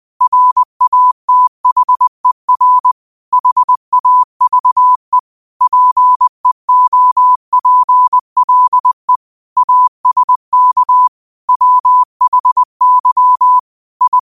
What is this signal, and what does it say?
Morse code, heard as a keyed tone, about 20 wpm, 1000 Hz. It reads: RATHER HAVE PEOPLE ASK WHY I
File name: qotdfiles / 20wpm